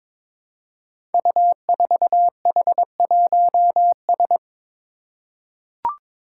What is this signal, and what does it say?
Morse code, heard as a keyed tone, about 22 wpm, 700 Hz. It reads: U4H1S
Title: Five Challenging Letter-Number Combinations 22wpm